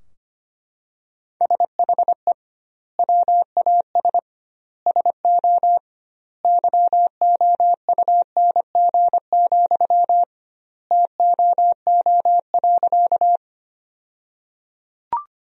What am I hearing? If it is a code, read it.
Morse code, heard as a keyed tone, about 25 wpm, 700 Hz. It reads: SHE WAS SO YOUNG, TOO.
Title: 2x Word-Spacing - Sentences from Top 400 Words - 25wpm